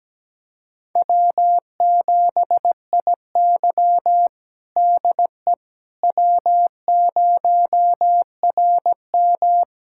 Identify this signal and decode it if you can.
Morse code, heard as a keyed tone, about 17 wpm, 700 Hz. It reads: W7IY DE W0RM
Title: USA Callsign DE Callsign - 4 Characters - 17wpm